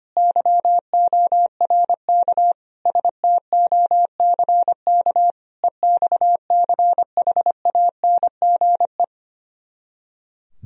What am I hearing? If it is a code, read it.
Morse code, heard as a keyed tone, about 29 wpm, 700 Hz. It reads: YORK STOCK EXCHANGE
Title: Speed-Racing - Sets of 5 Words 25wpm